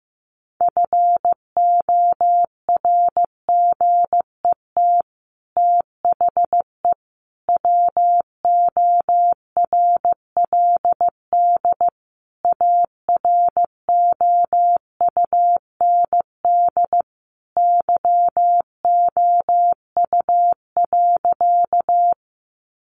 Morse code, heard as a keyed tone, about 15 wpm, 700 Hz. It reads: FORGET THE WORLD AROUND YOU.
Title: Sentences from Top 1200 Words 15wpm